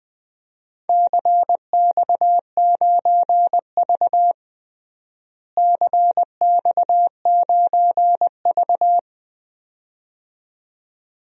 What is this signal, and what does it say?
Morse code, heard as a keyed tone, about 20 wpm, 700 Hz. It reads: CX9V CX9V
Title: DXCC-100 Call Signs - 4 Characters - Repeated 2x - 20wpm